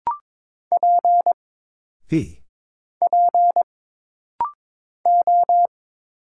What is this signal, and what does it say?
Morse code, heard as a keyed tone, about 22 wpm, 700 Hz. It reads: P P O
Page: This rapid-fire timing is geared to push your ICR (Instant Character Recognition) skills.